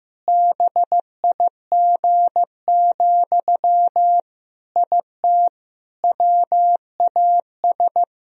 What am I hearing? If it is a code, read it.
Morse code, heard as a keyed tone, about 15 wpm, 700 Hz. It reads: BIG, IT WAS
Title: Sentences from Top 1000 Words - Review Part 3 - 15wpm